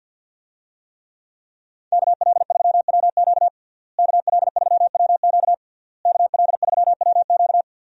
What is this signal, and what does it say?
Morse code, heard as a keyed tone, about 52 wpm, 700 Hz. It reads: KL3WQ KL3WQ KL3WQ
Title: USA Call Signs - 5 Characters - Repeated 3x - 50wpm